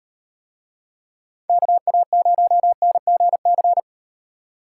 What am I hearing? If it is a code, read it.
Morse code, heard as a keyed tone, about 38 wpm, 700 Hz. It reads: KA0NGC